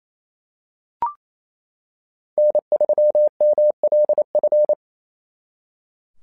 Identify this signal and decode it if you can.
Morse code, heard as a keyed tone, about 28 wpm, 600 Hz. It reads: N3MLF
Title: SOTA QSO Practice 28wpm